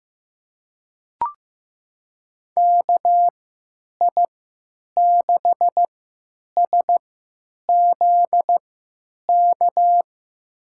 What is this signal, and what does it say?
Morse code, heard as a keyed tone, about 15 wpm, spaced as for 9 wpm, 700 Hz. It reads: KI6SZK
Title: USA Call Signs - 6 Characters - 3x Character Spacing - 15wpm